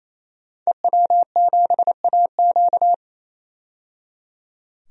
Morse code, heard as a keyed tone, about 27 wpm, 700 Hz. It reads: EW7AQ
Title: DXCC-100 Call Signs - 5 Characters - 28wpm